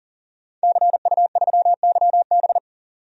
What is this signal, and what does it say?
Morse code, heard as a keyed tone, about 40 wpm, 700 Hz. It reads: CU3YB
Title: DXCC-100 Call Signs - 5 Characters - 40wpm